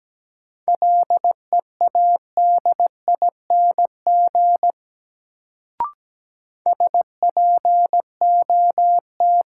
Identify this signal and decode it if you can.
Morse code, heard as a keyed tone, about 17 wpm, 700 Hz. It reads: LEADING SPOT
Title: Sentences from Top 1100 Words - New Word Drill - 17wpm